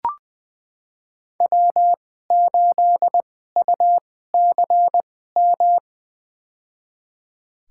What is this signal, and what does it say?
Morse code, heard as a keyed tone, about 20 wpm, spaced as for 14 wpm, 700 Hz. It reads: W8UCM
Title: USA Call Signs - 5 Characters - 2x Character Spacing - 20wpm